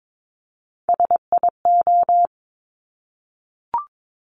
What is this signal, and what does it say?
Morse code, heard as a keyed tone, about 22 wpm, 690 Hz. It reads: SIO
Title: Top Three Letter Combinations in Words 22wpm